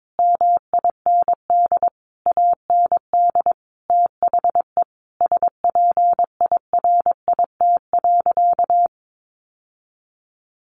Morse code, heard as a keyed tone, about 22 wpm, 700 Hz. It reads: MIND AND THE SPIRIT.
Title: Sentences from Top 1400 Words 22wpm